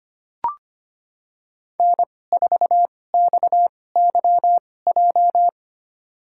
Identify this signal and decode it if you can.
Morse code, heard as a keyed tone, about 25 wpm, spaced as for 18 wpm, 700 Hz. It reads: N4XYJ
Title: USA Call Signs - 5 Characters - 2x Character Spacing - 25wpm